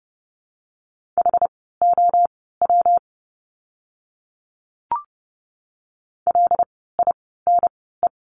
Morse code, heard as a keyed tone, about 30 wpm, spaced as for 17 wpm, 700 Hz. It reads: HOW LINE
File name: course / 30:10wpm / 115 Lesson 19f - Letter w - Words 30wpm